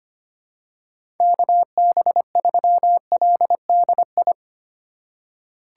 Morse code, heard as a keyed tone, about 25 wpm, 700 Hz. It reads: KB3LDI
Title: USA Call Signs - 6 Characters - 2x Character Spacing - 25wpm